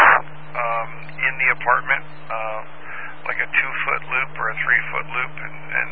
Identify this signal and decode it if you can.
Emergency Alert System (SAME) tones, heard as harsh buzzing bursts, with speech. It is heard from 1.1 to 4.9 s.